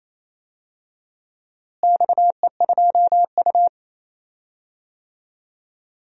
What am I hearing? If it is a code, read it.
Morse code, heard as a keyed tone, about 28 wpm, 700 Hz. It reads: XE2U
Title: DXCC-100 Call Signs - 4 Characters - Repeated 3x - 28wpm